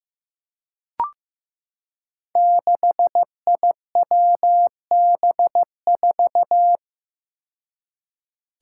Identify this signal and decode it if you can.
Morse code, heard as a keyed tone, about 15 wpm, 700 Hz. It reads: 6IWB4
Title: Five Challenging Letter-Number Combinations 15wpm